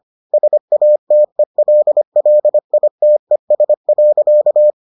Morse code, heard as a keyed tone, about 25 wpm, 600 Hz. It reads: SATELLITES.